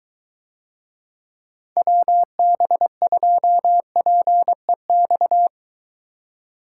Morse code, heard as a keyed tone, about 23 wpm, 700 Hz. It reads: WB2PEX